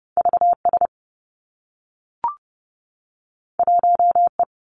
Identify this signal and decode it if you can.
Morse code, heard as a keyed tone, about 30 wpm, 700 Hz. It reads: VS 1E
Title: Two Challenging Letter-Number Combinations 30wpm